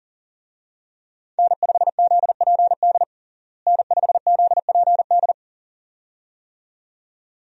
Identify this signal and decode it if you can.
Morse code, heard as a keyed tone, about 40 wpm, 700 Hz. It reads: N5ZPD N5ZPD
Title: USA Call Signs - 5 Characters - Repeated 2x - 40wpm